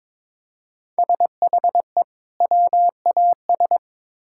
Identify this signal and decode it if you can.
Morse code, heard as a keyed tone, about 22 wpm, 700 Hz. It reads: SHE WAS
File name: Build A Phrase - 2-Word Phrases - 22wpm